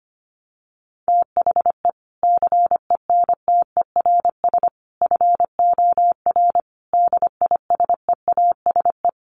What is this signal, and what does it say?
Morse code, heard as a keyed tone, about 25 wpm, 700 Hz. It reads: THE CENTERS FOR DISEASE
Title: Sets of 5 Words Encore2 25wpm